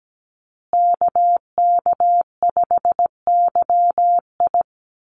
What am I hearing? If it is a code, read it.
Morse code, heard as a keyed tone, about 17 wpm, 700 Hz. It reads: KK5YI
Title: USA Call Signs Repeated 2x 17wpm